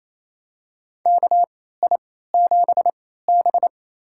Morse code, heard as a keyed tone, about 28 wpm, spaced as for 16 wpm, 700 Hz. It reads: KI7B
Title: USA Call Signs - 4 Characters - 3x Character Spacing - 28wpm